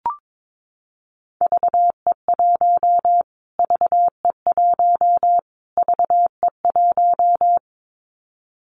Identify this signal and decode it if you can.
Morse code, heard as a keyed tone, about 22 wpm, 700 Hz. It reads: VE1 VE1 VE1
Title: Callsign Prefixes for Top 25 DXCC - Repeated 3x - 22wpm